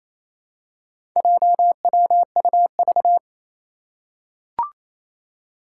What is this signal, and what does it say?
Morse code, heard as a keyed tone, about 28 wpm, 700 Hz. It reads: JWUV
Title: Four Challenging Letter-Number Combinations 28wpm